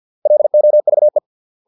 Morse code, heard as a keyed tone, about 50 wpm, 600 Hz. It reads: LOVE